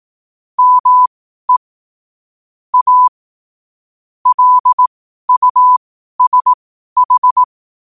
Morse code, heard as a keyed tone, about 18 wpm, spaced as for 12 wpm, 1000 Hz. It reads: ME A LUSH
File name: qotdfiles / 12wpm